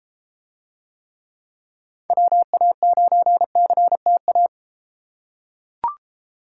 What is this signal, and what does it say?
Morse code, heard as a keyed tone, about 32 wpm, 700 Hz. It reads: WA9CTA